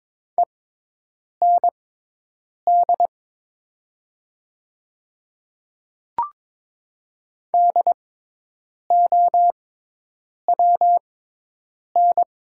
Morse code, heard as a keyed tone, about 23 wpm, spaced as for 8 wpm, 700 Hz. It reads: END DOWN
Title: 6x Character Spacing - Top 300 Words - 22wpm